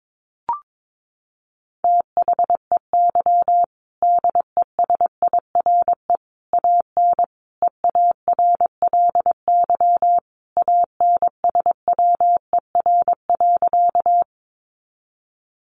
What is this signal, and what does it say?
Morse code, heard as a keyed tone, about 22 wpm, 700 Hz. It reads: THEY DESIRE AN EARLY ANSWER.
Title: Sentences from Top 1100 Words 22wpm